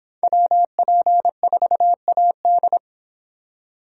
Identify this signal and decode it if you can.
Morse code, heard as a keyed tone, about 25 wpm, 700 Hz. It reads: WP4AD